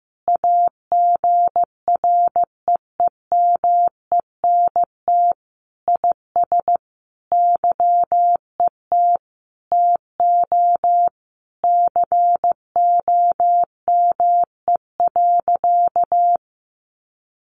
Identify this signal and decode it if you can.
Morse code, heard as a keyed tone, about 15 wpm, 700 Hz. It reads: AGREEMENT IS YET TO COME.
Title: Sentences from Top 1500 Words 15wpm